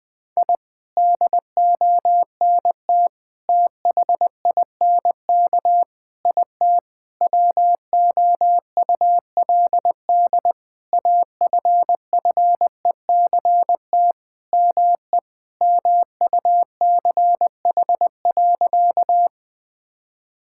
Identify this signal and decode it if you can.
Morse code, heard as a keyed tone, about 21 wpm, 700 Hz. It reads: I DONT THINK IT WOULD AFFECT ME MUCH.
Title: Sentences from Top 1000 Words - Review Part 6 - 20wpm